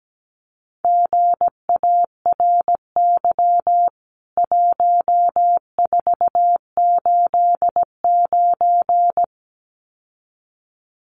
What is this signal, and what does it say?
Morse code, heard as a keyed tone, about 17 wpm, 700 Hz. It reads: GARY 1489
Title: CWT Exchange 17wpm